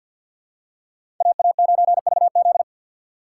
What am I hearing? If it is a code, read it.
Morse code, heard as a keyed tone, about 47 wpm, 700 Hz. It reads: AA9VZ
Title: USA Call Signs - 5 Characters - 3x Character Spacing - 50wpm